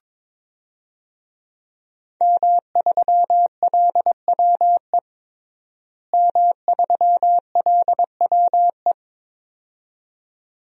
Morse code, heard as a keyed tone, about 22 wpm, 700 Hz. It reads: M3LWE M3LWE
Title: DXCC-100 Call Signs - 5 Characters - Repeated 2x - 22wpm